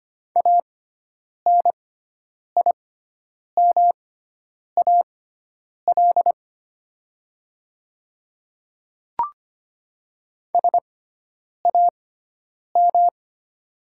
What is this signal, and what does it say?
Morse code, heard as a keyed tone, about 25 wpm, spaced as for 9 wpm, 700 Hz. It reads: ANIMAL SAM
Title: 6x Character Spacing - Top 500 Words - 25wpm